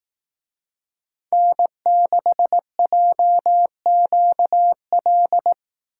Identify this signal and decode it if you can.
Morse code, heard as a keyed tone, about 18 wpm, 700 Hz. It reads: N6JQL
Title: Speed Racing - USA 1x3 Call Signs 15wpm